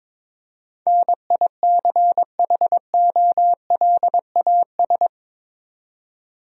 Morse code, heard as a keyed tone, about 22 wpm, 700 Hz. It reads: NICHOLAS
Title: USA Common Names - 22wpm